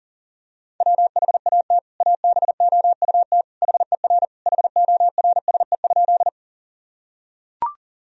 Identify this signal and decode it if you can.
Morse code, heard as a keyed tone, about 40 wpm, 700 Hz. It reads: WHAT ABOUT HER HORSE?
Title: Sentences from Top 1000 Words - Review Part 1 - 40wpm